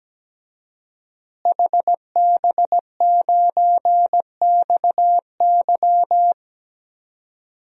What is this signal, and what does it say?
Morse code, heard as a keyed tone, about 17 wpm, 700 Hz. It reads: HB9XY
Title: DXCC-100 Call Signs - 5 Characters - Repeated 3x - 17wpm